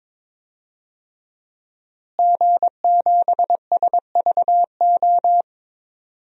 Morse code, heard as a keyed tone, about 22 wpm, 700 Hz. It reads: G7SVO
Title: DXCC-100 Call Signs - 5 Characters - Repeated 3x - 22wpm